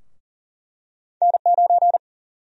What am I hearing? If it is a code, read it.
Morse code, heard as a keyed tone, about 40 wpm, 700 Hz. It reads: N9
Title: USA Call Sign Prefixes - 40wpm